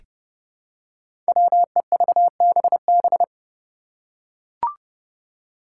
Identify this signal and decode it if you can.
Morse code, heard as a keyed tone, about 30 wpm, 700 Hz. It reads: WEVBB
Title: Five Challenging Letter-Number Combinations 30wpm